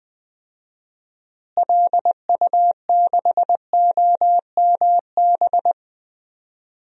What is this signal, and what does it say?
Morse code, heard as a keyed tone, about 19 wpm, 700 Hz. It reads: LU6OMB
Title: DXCC-100 Call Signs - 6 Characters - Repeated 3x - 20wpm